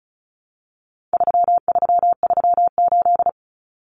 Morse code, heard as a keyed tone, about 34 wpm, 700 Hz. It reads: 3338
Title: Common Cut Numbers - 4-Digit 35wpm